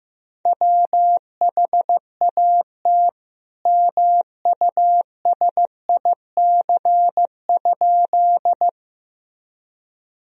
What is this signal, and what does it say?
Morse code, heard as a keyed tone, about 15 wpm, 700 Hz. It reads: WHAT MUSIC?
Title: Sentences from Top 1000 Words - Review Part 1 - 15wpm